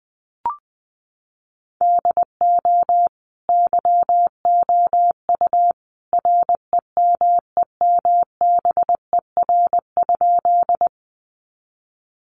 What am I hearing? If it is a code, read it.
Morse code, heard as a keyed tone, about 20 wpm, 700 Hz. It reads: DO YOU REMEMBER?
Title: Sentences from Top 400 Words 20wpm